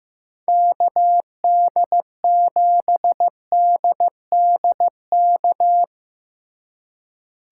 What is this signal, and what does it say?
Morse code, heard as a keyed tone, about 17 wpm, 700 Hz. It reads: KD7DDK
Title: Speed Racing - USA 2x3 Call Signs 15wpm